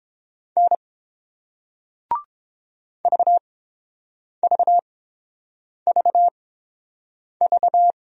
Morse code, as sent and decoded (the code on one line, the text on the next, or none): -. / ...- ...- ...- ...-
N VVVV